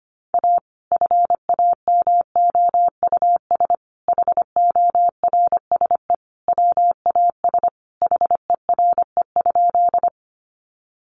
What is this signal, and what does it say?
Morse code, heard as a keyed tone, about 25 wpm, 700 Hz. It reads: A FAMOUS HORSE WAS HERE?
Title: Sentences from Top 1500 Words 25wpm